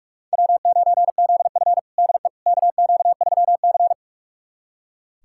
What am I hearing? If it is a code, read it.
Morse code, heard as a keyed tone, about 44 wpm, 700 Hz. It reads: W9ZF DE KQ3C
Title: USA Callsign DE Callsign - 4 Characters - 45wpm